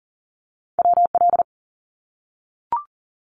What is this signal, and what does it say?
Morse code, heard as a keyed tone, about 42 wpm, 700 Hz. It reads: WL
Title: Abbreviations 40wpm